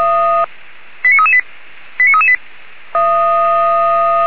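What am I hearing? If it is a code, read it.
Five-tone selective call (ZVEI), heard as quick runs of five tones, tones of about 70 ms. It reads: 89338 89338
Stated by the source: ZVEI five-tone call 89375 received by SDR.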